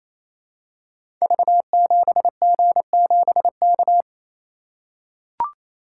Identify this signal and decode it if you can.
Morse code, heard as a keyed tone, about 27 wpm, 700 Hz. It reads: V7G7K
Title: Code Groups - 5 Letter:Numbers - 28wpm